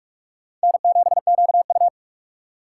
Morse code, heard as a keyed tone, about 45 wpm, 700 Hz. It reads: N7QU